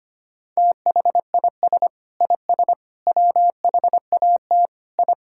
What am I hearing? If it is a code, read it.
Morse code, heard as a keyed tone, about 25 wpm, 700 Hz. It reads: THIS IS WHAT I
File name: practice / 25wpm / Sentences with 3-Word Phrases - 25wpm